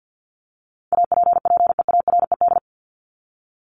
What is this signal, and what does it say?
Morse code, heard as a keyed tone, about 48 wpm, 700 Hz. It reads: APPEARED
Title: ED Suffix - Top 100 Words with Base Words 50wpm